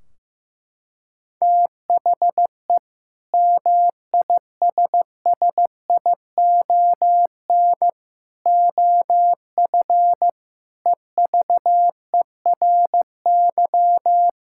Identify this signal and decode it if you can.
Morse code, heard as a keyed tone, about 15 wpm, 700 Hz. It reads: THE MISSION OF EVERY
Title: Sentences from Top 1600 Words 15wpm